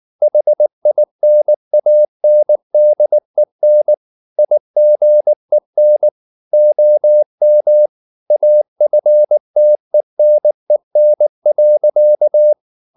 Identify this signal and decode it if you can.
Morse code, heard as a keyed tone, about 19 wpm, 600 Hz. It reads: HINANDEN IGEN OM AFTENEN.
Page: Ferie 19 wpm